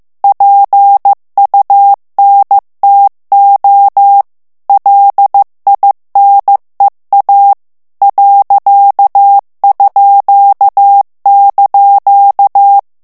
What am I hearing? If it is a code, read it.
Morse code, heard as a keyed tone, about 15 wpm, 800 Hz. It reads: PUNTO LINEA ._)
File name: ITALIEN texte clair 1 vitesse 14 mots minute